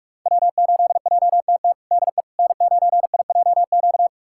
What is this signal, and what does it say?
Morse code, heard as a keyed tone, about 46 wpm, 700 Hz. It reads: W8JTT DE N9IJQ
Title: USA Callsign DE Callsign - 5 Characters - 45wpm